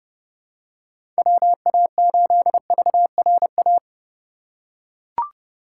Morse code, heard as a keyed tone, about 32 wpm, 700 Hz. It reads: WA8VRA
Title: USA Call Signs - 6 Characters - 3x Character Spacing - 30wpm